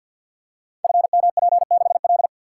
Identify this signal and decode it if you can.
Morse code, heard as a keyed tone, about 52 wpm, 700 Hz. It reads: FMP6L
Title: Code Groups - 5 Letter:Numbers - 50wpm